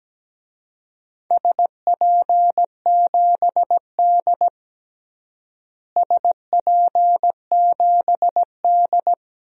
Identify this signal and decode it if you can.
Morse code, heard as a keyed tone, about 17 wpm, 700 Hz. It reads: SP7D SP7D
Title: DXCC-100 Call Signs - 4 Characters - Repeated 2x - 17wpm